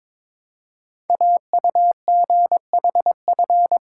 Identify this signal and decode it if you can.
Morse code, heard as a keyed tone, about 23 wpm, 700 Hz. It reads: AUGHF
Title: Code Groups - 5 Letters - 22wpm